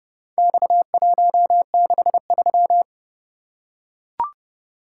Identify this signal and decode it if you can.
Morse code, heard as a keyed tone, about 27 wpm, 700 Hz. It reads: X163